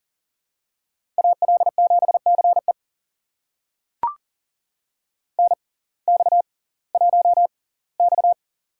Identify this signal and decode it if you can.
Morse code, heard as a keyed tone, about 39 wpm, spaced as for 22 wpm, 700 Hz. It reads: AL7CE NX1X
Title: USA Call Signs - 6x Character Spacing - 40wpm